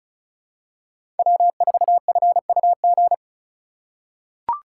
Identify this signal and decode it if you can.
Morse code, heard as a keyed tone, about 36 wpm, 700 Hz. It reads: W4FUG
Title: USA Call Signs - 5 Characters - 2x Character Spacing - 35wpm